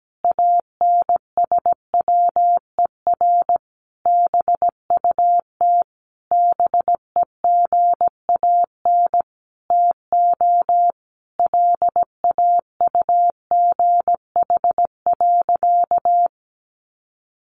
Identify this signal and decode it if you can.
Morse code, heard as a keyed tone, about 17 wpm, 700 Hz. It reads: ANSWER BUT BEGAN TO LAUGH.
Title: Sentences from Top 600 Words 17wpm